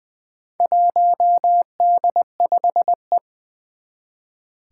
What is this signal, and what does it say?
Morse code, heard as a keyed tone, about 20 wpm, 700 Hz. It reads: 1D5E
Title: Four Challenging Letter-Number Combinations 20wpm